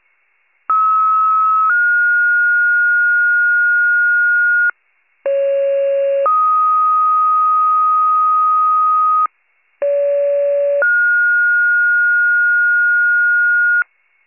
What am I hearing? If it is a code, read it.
Two-tone page, heard as a one-second tone, then a three-second tone, about 1290 then 1430 Hz, about 550 then 1250 Hz, about 570 then 1430 Hz, with noise. It is heard from 0.7 to 13.8 s.